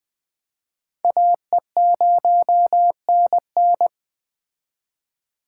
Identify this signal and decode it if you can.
Morse code, heard as a keyed tone, about 21 wpm, 700 Hz. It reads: AE0NN